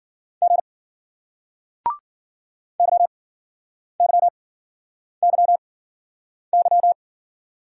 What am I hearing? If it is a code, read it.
Morse code, heard as a keyed tone, about 48 wpm, spaced as for 9 wpm, 700 Hz. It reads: G YYYY
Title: Speed-Racing Single Letters 40wpm